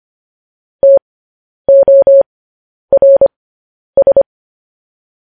Morse code, heard as a keyed tone, about 25 wpm, spaced as for 10 wpm, 550 Hz. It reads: TORS
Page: news-headlines-in-morse-code-at-10-wpm